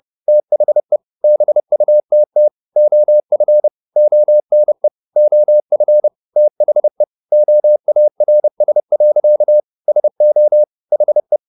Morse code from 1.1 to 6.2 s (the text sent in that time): BUTT OF ONE OF